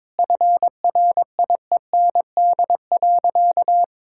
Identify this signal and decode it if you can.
Morse code, heard as a keyed tone, about 21 wpm, 700 Hz. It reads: FRIEND.
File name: practice / 22wpm / Sentences with 2-Word Phrases - 22wpm